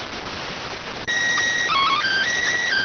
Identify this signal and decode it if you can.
SSTV (slow-scan television), PD 290.